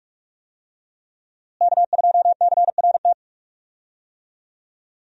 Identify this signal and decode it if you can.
Morse code, heard as a keyed tone, about 46 wpm, 700 Hz. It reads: K2CRT
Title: USA Call Signs - 5 Characters - Repeated 3x - 45wpm